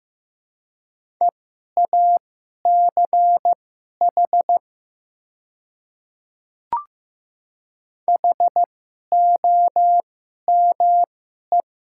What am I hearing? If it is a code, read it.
Morse code, heard as a keyed tone, about 15 wpm, spaced as for 11 wpm, 700 Hz. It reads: EACH HOME
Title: 2x Character Spacing - Top 500 Words - 15wpm